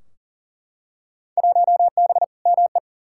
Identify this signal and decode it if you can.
Morse code, heard as a keyed tone, about 37 wpm, 700 Hz. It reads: 1B ME
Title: Field Day Exchanges - 40wpm